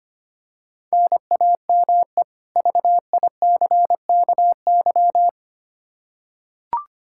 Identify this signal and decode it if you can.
Morse code, heard as a keyed tone, about 25 wpm, 700 Hz. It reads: NAME VICKY
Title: Practice QSO - Exchange 1 - Beginning - 25wpm